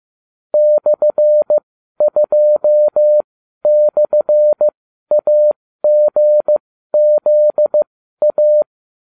Morse code, heard as a keyed tone, about 15 wpm, spaced as for 12 wpm, 600 Hz. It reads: /2/AGZA